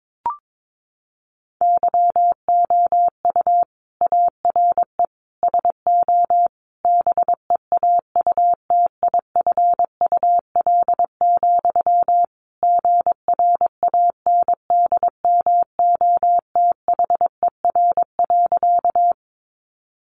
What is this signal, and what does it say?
Morse code, heard as a keyed tone, about 22 wpm, 700 Hz. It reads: YOU ARE SO BEAUTIFUL, GRANDMOTHER.
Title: Sentences from Top 1200 Words 22wpm